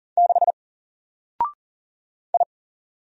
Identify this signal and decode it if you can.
Morse code, heard as a keyed tone, about 40 wpm, 700 Hz. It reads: 6 I
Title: Single Character (Letter:Number:Punctuation:Prosign) 40wpm